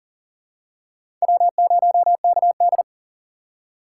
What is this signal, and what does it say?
Morse code, heard as a keyed tone, about 38 wpm, 700 Hz. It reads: W0KD
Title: USA Call Signs - 4 Characters - 3x Character Spacing - 40wpm